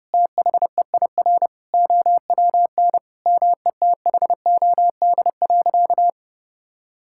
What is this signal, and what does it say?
Morse code, heard as a keyed tone, about 32 wpm, 700 Hz. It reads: THEIR OWN METHOD.
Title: Sentences from Top 1000 Words - Review Part 6 - 30wpm